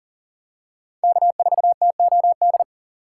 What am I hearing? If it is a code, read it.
Morse code, heard as a keyed tone, about 38 wpm, 690 Hz. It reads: K4TOD
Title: USA Call Signs - 5 Characters - 3x Character Spacing - 40wpm